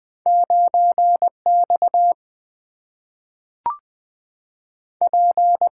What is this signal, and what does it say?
Morse code, heard as a keyed tone, about 20 wpm, 700 Hz. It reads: 9 X P
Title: Two Letter-Number Combinations - 20wpm